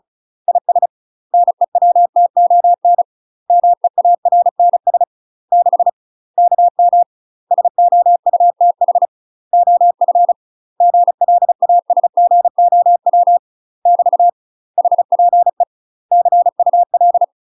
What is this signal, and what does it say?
Morse code, heard as a keyed tone, about 34 wpm, 700 Hz. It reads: IS NEWTON MEARNS 6 KM SOUTH OF GLASGOW <BT> HPE CUL
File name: UK MASTER QSOs TRAINING / 35WPM / UK MASTER QSOs 0001